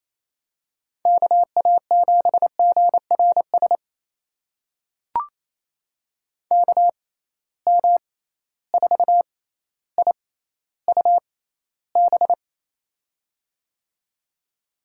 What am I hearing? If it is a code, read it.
Morse code, heard as a keyed tone, about 28 wpm, spaced as for 14 wpm, 700 Hz. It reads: KA7GRS KM4IUB
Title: USA Call Signs - 6 Characters - 6x Character Spacing - 28wpm